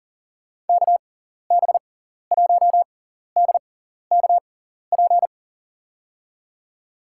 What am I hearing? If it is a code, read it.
Morse code, heard as a keyed tone, about 40 wpm, spaced as for 14 wpm, 700 Hz. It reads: KB1DKP
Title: USA Call Signs - 6x Character Spacing - 40wpm